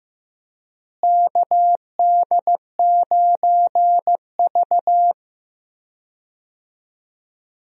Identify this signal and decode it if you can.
Morse code, heard as a keyed tone, about 16 wpm, 700 Hz. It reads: KD9V
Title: USA Call Signs - 4 Characters - Repeated 2x - 15wpm